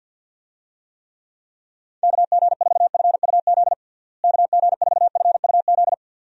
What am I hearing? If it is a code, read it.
Morse code, heard as a keyed tone, about 52 wpm, 700 Hz. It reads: KG4FUZ KG4FUZ
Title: USA Call Signs - 6 Characters - Repeated 2x - 50wpm